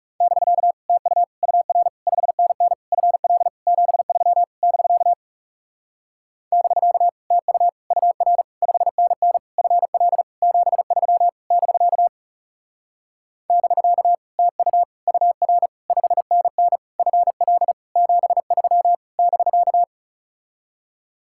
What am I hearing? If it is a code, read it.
Morse code, heard as a keyed tone, about 37 wpm, 700 Hz. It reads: <BK> TU UR 5NN FL 73 <BK> <BK> TU UR 5NN FL 73 <BK> <BK> TU UR 5NN FL 73 <BK>
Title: Speed-Racing Common QSO Phrases Encore - 30wpm